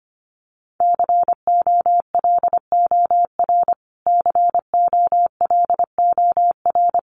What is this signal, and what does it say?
Morse code, heard as a keyed tone, about 25 wpm, 700 Hz. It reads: COLOR COLOR
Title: Top 500 Words Repeated 25wpm